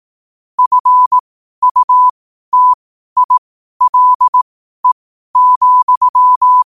Morse code, heard as a keyed tone, about 18 wpm, spaced as for 12 wpm, 1000 Hz. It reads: FUTILE,
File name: qotdfiles / 12wpm